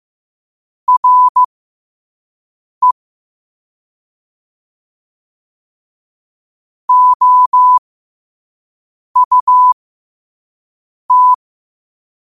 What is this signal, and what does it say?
Morse code, heard as a keyed tone, about 15 wpm, spaced as for 5 wpm, 1000 Hz. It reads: RE OUT
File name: qotdfiles / 5wpm